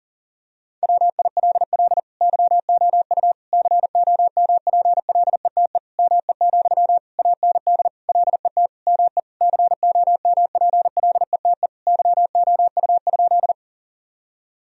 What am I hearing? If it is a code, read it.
Morse code, heard as a keyed tone, about 40 wpm, 700 Hz. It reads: WILL YOU COMPLETE ME, AND LET ME COMPLETE YOU?
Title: Sentences from Top 1000 Words - Review Part 2 - 40wpm